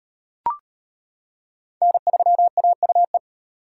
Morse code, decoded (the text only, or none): N3AUE